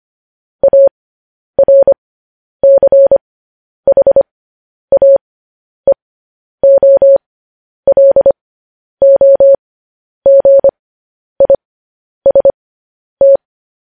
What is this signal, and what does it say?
Morse code, heard as a keyed tone, about 25 wpm, spaced as for 10 wpm, 550 Hz. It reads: ARCHAEOLOGIST
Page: news-headlines-in-morse-code-at-10-wpm